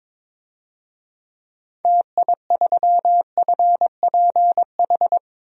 Morse code, heard as a keyed tone, about 22 wpm, 700 Hz. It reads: TI3FPH
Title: DXCC-100 Call Signs - 6 Characters - 22wpm